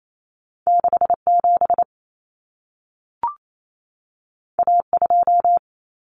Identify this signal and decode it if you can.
Morse code, heard as a keyed tone, about 28 wpm, 700 Hz. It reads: 67 A2
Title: Common Cut Numbers - 2-Digit 28wpm